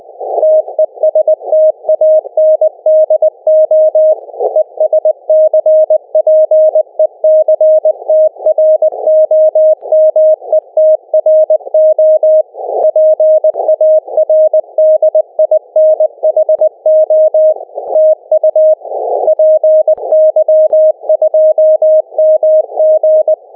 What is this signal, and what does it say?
Morse code, heard as a keyed tone, about 20 wpm, 620 Hz. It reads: TESTANDO ESCPECTROMETRO PARDINHO TU PY2MG
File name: py2mg / Arquivos PDF / Transceptor fenix RX PArdinho